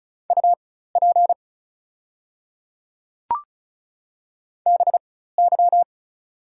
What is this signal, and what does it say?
Morse code, heard as a keyed tone, about 34 wpm, spaced as for 16 wpm, 700 Hz. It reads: UP BY
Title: 4x Character Spacing - Top 100 Words - 35wpm